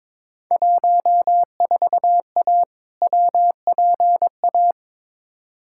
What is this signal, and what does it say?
Morse code, heard as a keyed tone, about 22 wpm, 700 Hz. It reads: 14A WPA
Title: Field Day Exchanges Encore - 22wpm